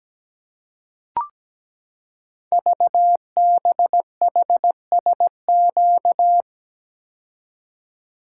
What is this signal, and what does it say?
Morse code, heard as a keyed tone, about 17 wpm, 700 Hz. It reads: VBHSQ
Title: Code Groups - 5 Letter:Numbers - 17wpm